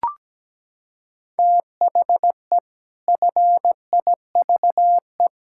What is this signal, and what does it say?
Morse code, heard as a keyed tone, about 17 wpm, 700 Hz. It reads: THE FIVE
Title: Pangram Challenge v2 - 17wpm